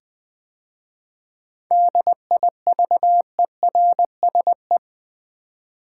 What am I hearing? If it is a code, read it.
Morse code, heard as a keyed tone, about 20 wpm, 700 Hz. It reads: DIVERSE
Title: 7 Letter Words - First 3 Letters Given - 20wpm